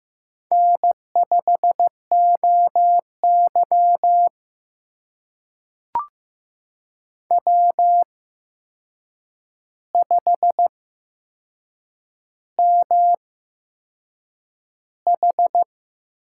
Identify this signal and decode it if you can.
Morse code, heard as a keyed tone, about 15 wpm, spaced as for 6 wpm, 700 Hz. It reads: N5OY W5MH
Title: USA Call Signs - 4 Characters - 8x Character Spacing - 15wpm